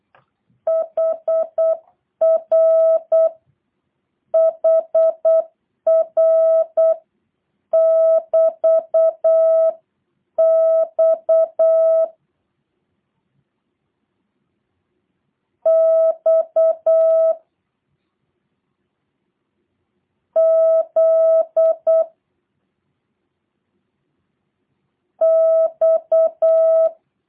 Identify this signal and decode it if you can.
Morse code, heard as a keyed tone, about 8 wpm, spaced as for 7 wpm, 650 Hz. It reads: HRHR<BT>X X Z X